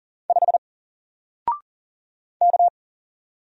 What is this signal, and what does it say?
Morse code, heard as a keyed tone, about 40 wpm, 700 Hz. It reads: 5 K
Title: Single Letter-Number 40wpm